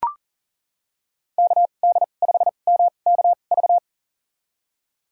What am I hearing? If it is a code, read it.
Morse code, heard as a keyed tone, about 40 wpm, spaced as for 29 wpm, 700 Hz. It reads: KD5MKV
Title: USA Call Signs - 2x Character Spacing - 40wpm